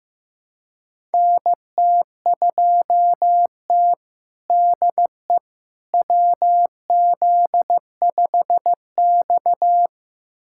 Morse code, heard as a keyed tone, about 15 wpm, 700 Hz. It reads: NT2T DE WZ5X
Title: USA Callsign DE Callsign - 4 Characters - 15wpm